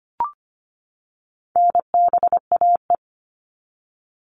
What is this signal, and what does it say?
Morse code, heard as a keyed tone, about 25 wpm, 700 Hz. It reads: NBAE
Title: All Cut Numbers - 4-Digit 25wpm